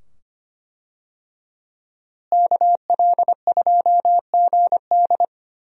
Morse code, heard as a keyed tone, about 25 wpm, 700 Hz. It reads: KL2GD
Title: USA Call Signs - 5 Characters - Repeated 2x - 25wpm